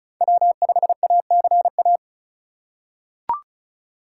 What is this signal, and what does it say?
Morse code, heard as a keyed tone, about 36 wpm, 700 Hz. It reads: W5ACA
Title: USA Call Signs - 5 Characters - 4x Character Spacing - 35wpm